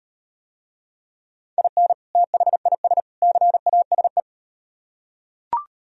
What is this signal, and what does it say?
Morse code, heard as a keyed tone, about 38 wpm, 700 Hz. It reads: IN THIS CASE